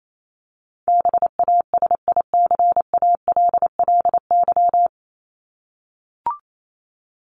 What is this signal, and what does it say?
Morse code, heard as a keyed tone, about 28 wpm, 700 Hz. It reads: BASICALLY
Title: Top 1251-1500 Words 28wpm